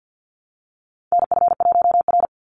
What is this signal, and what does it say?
Morse code, heard as a keyed tone, about 50 wpm, 700 Hz. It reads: NF1R
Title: NA SOTA Chaser Callsigns 50wpm